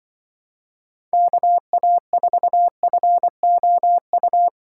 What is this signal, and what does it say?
Morse code, heard as a keyed tone, about 24 wpm, 700 Hz. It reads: KA4FOU
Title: Speed Racing - USA 2x3 Call Signs 20wpm